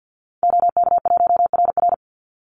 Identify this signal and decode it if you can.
Morse code, heard as a keyed tone, about 50 wpm, 700 Hz. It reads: OK1RR
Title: Call Signs - 50wpm